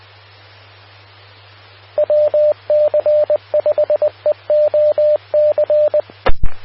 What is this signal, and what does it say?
Morse code, heard as a keyed tone, about 20 wpm, 600 Hz, with radio noise. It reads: WC5EOC
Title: Austin, TX Amateur Repeaters